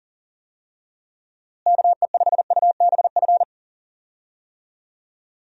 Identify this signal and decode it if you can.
Morse code, heard as a keyed tone, about 40 wpm, 700 Hz. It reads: KE5UBF
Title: USA Call Signs - 6 Characters - Repeated 2x - 40wpm